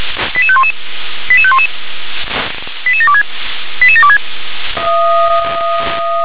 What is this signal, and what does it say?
Five-tone selective call (ZVEI), heard as quick runs of five tones, tones of about 70 ms. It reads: 88511 88511 88726 88726